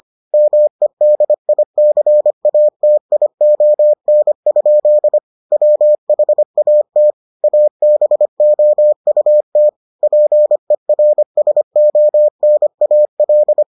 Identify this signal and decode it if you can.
Morse code, heard as a keyed tone, about 25 wpm, 600 Hz. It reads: MEDICATION? WHAT ABOUT PERSONAL